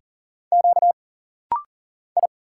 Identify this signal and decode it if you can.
Morse code, heard as a keyed tone, about 40 wpm, 700 Hz. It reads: Q I
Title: Single Letter-Number - Rapid-Fire 40wpm